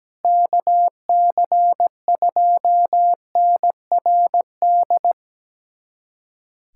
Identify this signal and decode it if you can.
Morse code, heard as a keyed tone, about 17 wpm, 700 Hz. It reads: KC2NRD
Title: USA Callsign DE Callsign 17wpm